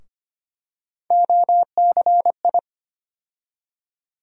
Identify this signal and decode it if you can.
Morse code, heard as a keyed tone, about 25 wpm, 700 Hz. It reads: OCI
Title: USA Call Sign Postfixes - 3 Letters 25wpm